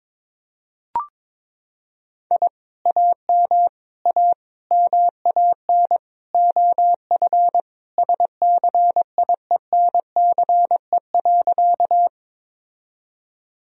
Morse code, heard as a keyed tone, about 22 wpm, 700 Hz. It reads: I AM A MAN OF SCIENCE.
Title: Sentences from Top 1000 Words - Review Part 2 - 22wpm